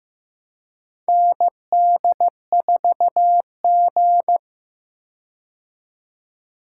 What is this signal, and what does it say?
Morse code, heard as a keyed tone, about 15 wpm, 700 Hz. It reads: ND4G
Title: K1USN SST QSO Practice - 15wpm